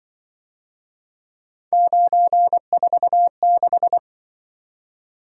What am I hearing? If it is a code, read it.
Morse code, heard as a keyed tone, about 25 wpm, 700 Hz. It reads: 946